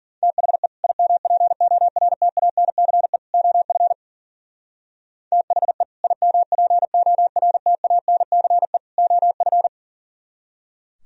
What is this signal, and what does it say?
Morse code, heard as a keyed tone, about 51 wpm, 700 Hz. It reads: THE IMPORTANCE OF THE IMPORTANCE OF